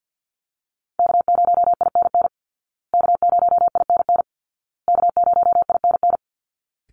Morse code, heard as a keyed tone, about 50 wpm, 700 Hz. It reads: K0INN K0INN K0INN
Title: Call Signs Repeated 2x - 50wpm